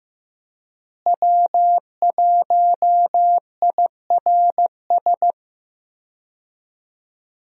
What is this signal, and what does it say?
Morse code, heard as a keyed tone, about 15 wpm, 700 Hz. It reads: W1IRS
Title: USA Call Signs - 5 Characters - Repeated 2x - 15wpm